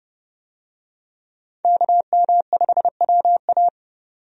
Morse code, heard as a keyed tone, about 30 wpm, 700 Hz. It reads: KM5WA
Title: USA Call Signs - 5 Characters - 30wpm